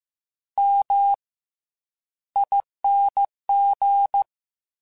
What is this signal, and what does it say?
Morse code, heard as a keyed tone, about 15 wpm, 800 Hz. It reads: M ING
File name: Special INGs-mp3s / Special INGs-mp3s / Fast ing @13